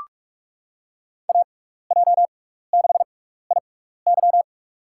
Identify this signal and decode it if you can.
Morse code, heard as a keyed tone, about 46 wpm, spaced as for 16 wpm, 700 Hz. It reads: AJ6IY
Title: USA Call Signs - 5 Characters - 6x Character Spacing - 45wpm